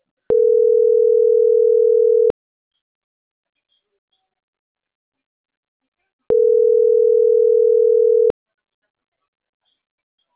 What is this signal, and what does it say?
Ringback tone (North America), 440 + 480 Hz.